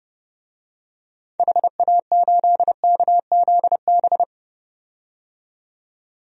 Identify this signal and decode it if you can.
Morse code, heard as a keyed tone, about 30 wpm, 700 Hz. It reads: HA8KZB
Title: DXCC-100 Call Signs - 6 Characters - 30wpm